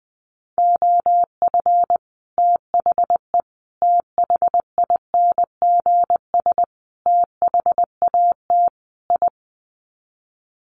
Morse code, heard as a keyed tone, about 20 wpm, 700 Hz. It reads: OF THE THINGS THAT I
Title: Sets of 5 Words Encore 20wpm